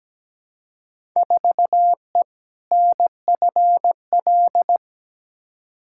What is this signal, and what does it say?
Morse code, heard as a keyed tone, about 17 wpm, 700 Hz. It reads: 4E NFL
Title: Field Day Exchanges Encore - 17wpm